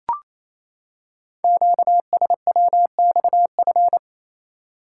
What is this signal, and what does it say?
Morse code, heard as a keyed tone, about 29 wpm, 700 Hz. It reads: QSWXF
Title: Code Groups - 5 Letters - 28wpm